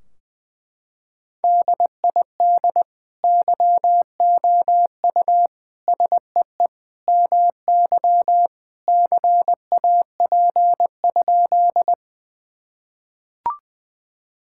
Morse code, heard as a keyed tone, about 20 wpm, 700 Hz. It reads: DID YOU SEE MY CAP?
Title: Sentences from Top 1600 Words 20wpm